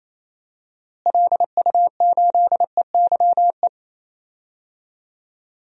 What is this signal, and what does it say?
Morse code, heard as a keyed tone, about 28 wpm, 700 Hz. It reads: LU8EYE
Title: DXCC-100 Call Signs - 6 Characters - Repeated 2x - 28wpm